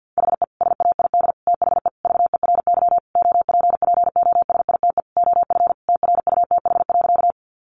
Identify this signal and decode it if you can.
Morse code, heard as a keyed tone, about 50 wpm, 700 Hz. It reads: HE SAID THE VERY OPPOSITE OF TRUTH.
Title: Sentences from Top 1300 Words 50wpm